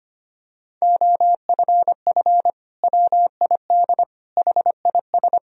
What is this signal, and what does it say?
Morse code, heard as a keyed tone, about 25 wpm, 700 Hz. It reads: OFF WID HIS
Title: Sentences from Top 300 Words and Abbreviations 25wpm